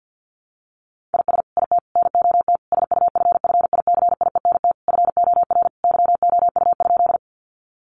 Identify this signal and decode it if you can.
Morse code, heard as a keyed tone, about 50 wpm, 700 Hz. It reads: IS IT NOT SUFFICIENT FOR YOU?
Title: Sentences from Top 1400 Words 50wpm